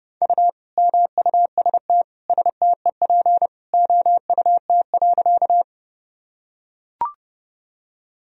Morse code, heard as a keyed tone, about 30 wpm, 700 Hz. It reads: U MUST STEP OUT.
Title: Sentences from Top 400 Words and Abbreviations 30wpm